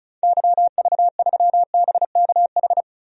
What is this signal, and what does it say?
Morse code, heard as a keyed tone, about 36 wpm, 690 Hz. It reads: YV3BKH
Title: DXCC-100 Call Signs - 6 Characters - 35wpm